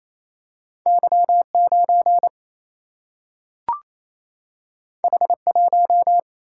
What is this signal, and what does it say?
Morse code, heard as a keyed tone, about 28 wpm, 700 Hz. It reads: Y 9 H 1
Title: Single-Letter Single-Number 28wpm